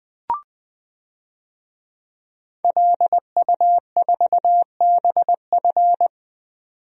Morse code, heard as a keyed tone, about 21 wpm, 700 Hz. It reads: LU4BF
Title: DXCC-100 Call Signs - 5 Characters - 20wpm